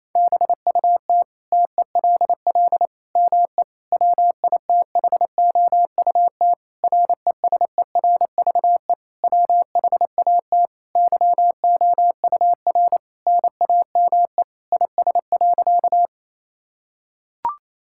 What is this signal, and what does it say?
Morse code, heard as a keyed tone, about 28 wpm, 700 Hz. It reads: BUT TELL ME WITHOUT RESERVE WHAT YOUR NAME IS.
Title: Sentences from Top 1000 Words - Review Part 6 - 28wpm